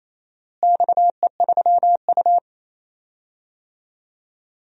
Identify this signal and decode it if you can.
Morse code, heard as a keyed tone, about 28 wpm, 700 Hz. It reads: XE3U
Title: DXCC-100 Call Signs - 4 Characters - Repeated 2x - 28wpm